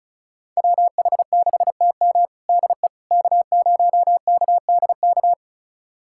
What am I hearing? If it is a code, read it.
Morse code, heard as a keyed tone, about 36 wpm, 700 Hz. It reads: WH6TM DE K0KDK
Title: USA Callsign DE Callsign - 5 Characters - 35wpm